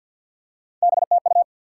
Morse code, heard as a keyed tone, about 53 wpm, 700 Hz. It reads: BTU
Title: Top 90 QSO Elements 50wpm